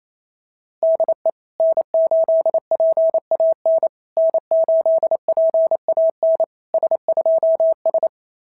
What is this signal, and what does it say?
Morse code, heard as a keyed tone, about 28 wpm, 660 Hz. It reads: DE N8PAN N8PAN S2S
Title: SOTA QSO Practice 28wpm